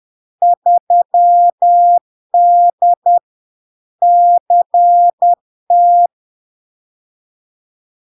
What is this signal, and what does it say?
Morse code, heard as a keyed tone, about 10 wpm, 700 Hz. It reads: 3D CT
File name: LARK-FD-2022-Log-in-CW-10wpm-700-Hz